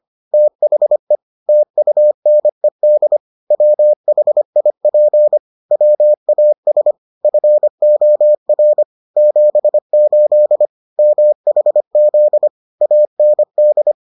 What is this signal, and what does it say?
Morse code, heard as a keyed tone, about 25 wpm, 600 Hz. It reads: THE TUNED WHIP WAS FOR 78 MHZ AND